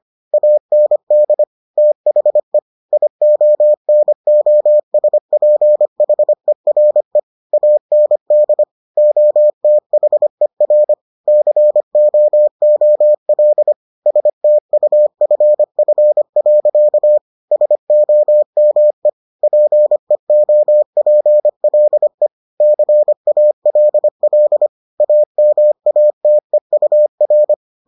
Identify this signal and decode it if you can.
Morse code, heard as a keyed tone, about 25 wpm, 600 Hz. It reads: AND THE IONOSPHERE AND OTHER COOL STUFF. SOME PEOPLE CALL AMATEUR